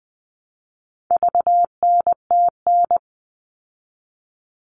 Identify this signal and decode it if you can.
Morse code, heard as a keyed tone, about 20 wpm, 700 Hz. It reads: VNTN
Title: All Cut Numbers - 4-Digit 20wpm